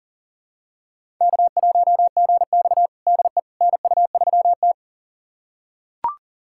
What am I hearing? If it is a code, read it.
Morse code, heard as a keyed tone, about 40 wpm, 700 Hz. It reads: K1GX DE NU3T
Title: USA Callsign DE Callsign - 4 Characters - 40wpm